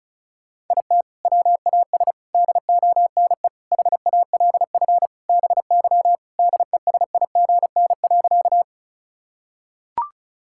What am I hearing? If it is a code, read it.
Morse code, heard as a keyed tone, about 34 wpm, 700 Hz. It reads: IT WAS DONE HALF BY DESIGN.
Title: Sentences from Top 1000 Words - Review Part 5 - 35wpm